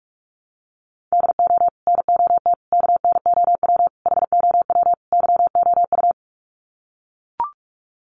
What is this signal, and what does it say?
Morse code, heard as a keyed tone, about 44 wpm, 700 Hz. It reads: DO NOT KNOW HOW YOU
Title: Sets of 5 Words Encore 45wpm